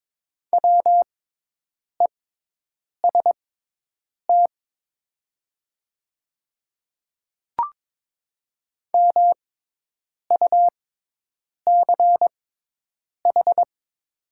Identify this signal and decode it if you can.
Morse code, heard as a keyed tone, about 22 wpm, spaced as for 8 wpm, 700 Hz. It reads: WEST MUCH
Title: 6x Character Spacing - Top 500 Words - 22wpm